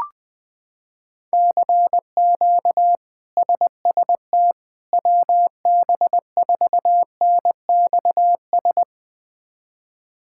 Morse code, heard as a keyed tone, about 20 wpm, 700 Hz. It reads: CQ SST WB4NXS
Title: K1USN SST QSO Practice - 20wpm